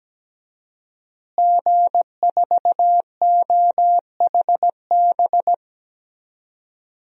Morse code, heard as a keyed tone, about 17 wpm, 700 Hz. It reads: G4OHB
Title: DXCC-100 Call Signs - 5 Characters - Repeated 2x - 17wpm